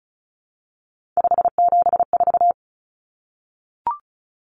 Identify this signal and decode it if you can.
Morse code, heard as a keyed tone, about 34 wpm, 700 Hz. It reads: 574
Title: Numbers - 3-Digit 35wpm